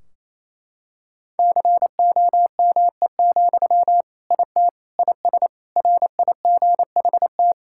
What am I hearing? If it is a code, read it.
Morse code, heard as a keyed tone, about 28 wpm, 700 Hz. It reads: COME, IT IS RIGHT
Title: Sentences from Top 1600 Words 28wpm